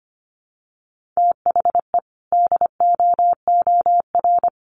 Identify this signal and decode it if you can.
Morse code, heard as a keyed tone, about 25 wpm, 700 Hz. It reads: THE DOOR
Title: Sentences from Top 1400 Words 25wpm